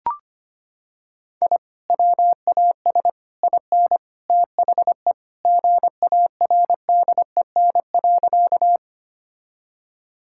Morse code, heard as a keyed tone, about 25 wpm, 700 Hz. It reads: I WAS IN THE GARDEN.
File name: practice / 25wpm / Sentences with 3-Word Phrases - 25wpm